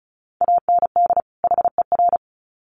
Morse code, heard as a keyed tone, about 34 wpm, 700 Hz. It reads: AND HER
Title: Sets of 2 Words 35wpm